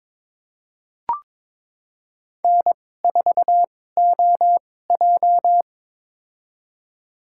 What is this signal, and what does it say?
Morse code, heard as a keyed tone, about 22 wpm, spaced as for 16 wpm, 700 Hz. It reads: N4OJ
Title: USA Call Signs - 2x Character Spacing - 22wpm